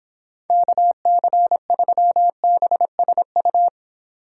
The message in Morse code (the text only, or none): KC3BSU